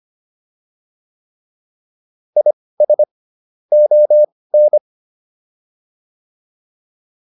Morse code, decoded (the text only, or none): IS ON